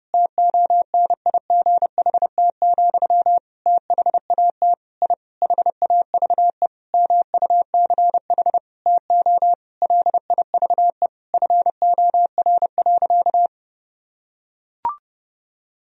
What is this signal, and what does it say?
Morse code, heard as a keyed tone, about 30 wpm, 700 Hz. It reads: TONIGHT, THAT I HAVE MUCH TO LIVE FOR.
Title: Sentences from Top 1000 Words - Review Part 3 - 30wpm